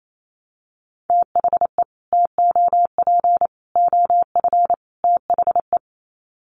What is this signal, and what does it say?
Morse code, heard as a keyed tone, about 28 wpm, 700 Hz. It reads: THE TOP OF THE
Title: Sets of 4 Words 28wpm